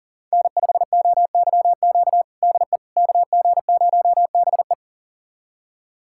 Morse code, heard as a keyed tone, about 40 wpm, 700 Hz. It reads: N5OYQ DE KG0BE
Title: USA Callsign DE Callsign 40wpm